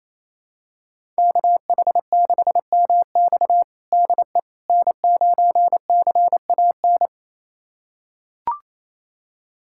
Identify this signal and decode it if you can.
Morse code, heard as a keyed tone, about 28 wpm, 700 Hz. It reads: KH6MX DE N9CAN
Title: USA Callsign DE Callsign - 5 Characters - 28wpm